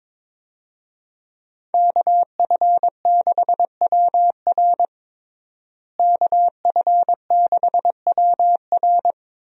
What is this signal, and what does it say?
Morse code, heard as a keyed tone, about 22 wpm, 700 Hz. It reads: KF6WR KF6WR
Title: USA Call Signs - 5 Characters - Repeated 2x - 22wpm